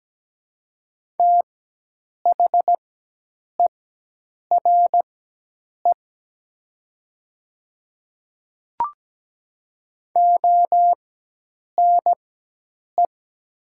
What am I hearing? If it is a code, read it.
Morse code, heard as a keyed tone, about 17 wpm, spaced as for 8 wpm, 700 Hz. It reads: THERE ONE
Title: 4x Character Spacing - Top 100 Words - 17wpm